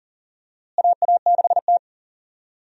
Morse code, heard as a keyed tone, about 40 wpm, 700 Hz. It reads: AA6T
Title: USA Call Signs - 4 Characters - Repeated 2x - 40wpm